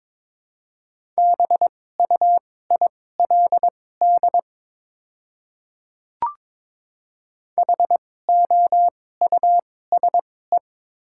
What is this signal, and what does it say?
Morse code, heard as a keyed tone, about 21 wpm, spaced as for 16 wpm, 700 Hz. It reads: BUILD HOUSE
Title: 2x Character Spacing - Top 300 Words - 22wpm